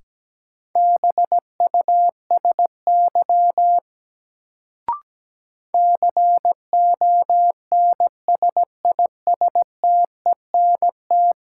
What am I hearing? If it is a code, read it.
Morse code, heard as a keyed tone, about 17 wpm, 700 Hz. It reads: BUSY CONSISTENT
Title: Sentences from Top 1500 Words - New Word Drill Alt - 17wpm